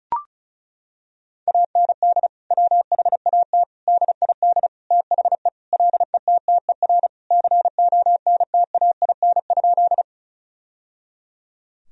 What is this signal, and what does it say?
Morse code, heard as a keyed tone, about 36 wpm, 700 Hz. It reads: AND WHAT DID THE LETTER CONTAIN?
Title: Sentences from Top 1000 Words - Review Part 2 - 35wpm